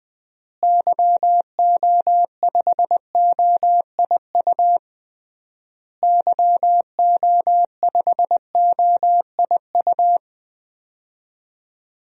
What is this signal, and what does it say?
Morse code, heard as a keyed tone, about 20 wpm, 700 Hz. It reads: YO5OIU YO5OIU
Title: DXCC-100 Call Signs - 6 Characters - Repeated 3x - 20wpm